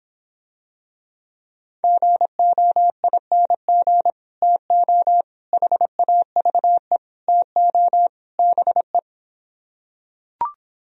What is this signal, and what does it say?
Morse code, heard as a keyed tone, about 26 wpm, 700 Hz. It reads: GOING TO HAVE TO BE